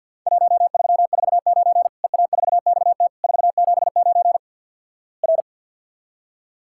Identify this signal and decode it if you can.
Morse code, heard as a keyed tone, about 50 wpm, 700 Hz. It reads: 1349 EA4XT 479 R
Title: Worked All Europe DX Contest - QTC Exchange - 50wpm